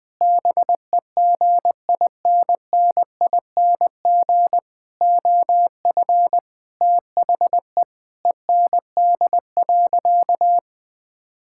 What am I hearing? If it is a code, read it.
Morse code, heard as a keyed tone, about 21 wpm, 700 Hz. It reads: BEGINNING OF THE END.
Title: Sentences from Top 1000 Words - Review Part 5 - 20wpm